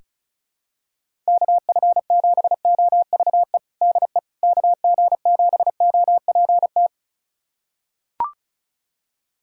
Morse code, heard as a keyed tone, about 36 wpm, 700 Hz. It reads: KF7OVE DE KG7OPT
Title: USA Callsign DE Callsign - 6 Characters - 35wpm